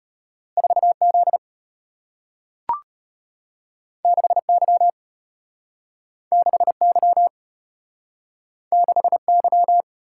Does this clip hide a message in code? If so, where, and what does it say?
Yes: Morse code from 0.6 to 9.8 s, heard as a keyed tone, about 34 wpm, 700 Hz. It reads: 4Z 6Y 6Y 6Y